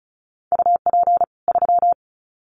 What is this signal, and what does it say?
Morse code, heard as a keyed tone, about 34 wpm, 700 Hz. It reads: UP 3
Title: Common QSO Phrases 35wpm